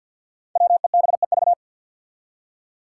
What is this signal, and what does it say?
Morse code, heard as a keyed tone, about 47 wpm, 700 Hz. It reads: WEBEV
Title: Five Challenging Letter-Number Combinations 50wpm